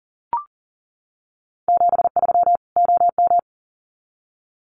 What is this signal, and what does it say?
Morse code, heard as a keyed tone, about 40 wpm, 700 Hz. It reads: 73 OM
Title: Common QSO Phrases 40wpm